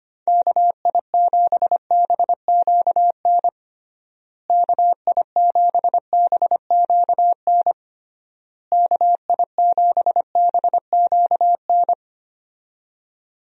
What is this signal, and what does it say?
Morse code, heard as a keyed tone, about 25 wpm, 700 Hz. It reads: KI7BQN KI7BQN KI7BQN
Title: USA Call Signs - 6 Characters - Repeated 3x - 25wpm